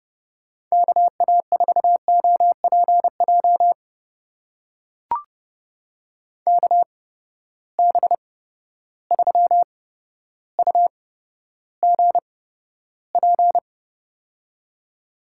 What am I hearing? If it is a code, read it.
Morse code, heard as a keyed tone, about 32 wpm, spaced as for 13 wpm, 700 Hz. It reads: KA4OPJ KB3UGP
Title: USA Call Signs - 8x Character Spacing - 30wpm